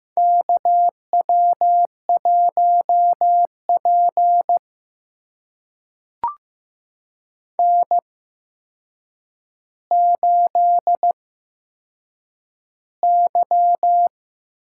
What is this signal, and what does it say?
Morse code, heard as a keyed tone, about 15 wpm, spaced as for 7 wpm, 700 Hz. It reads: KW1P N8Y
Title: USA Call Signs - 4 Characters - 8x Character Spacing - 15wpm